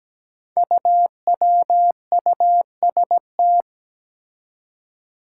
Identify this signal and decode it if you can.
Morse code, heard as a keyed tone, about 17 wpm, 700 Hz. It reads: UWUST
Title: Five Challenging Letter-Number Combinations 17wpm